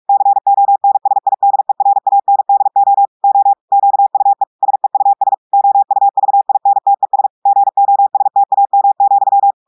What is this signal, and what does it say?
Morse code, heard as a keyed tone, about 46 wpm, 800 Hz. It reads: CONSIDERANDO O QUE SEUS OUVINTES GOSTAM,
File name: audios / portugais / portugais texte en clair 2 vitesse 45 mots minutes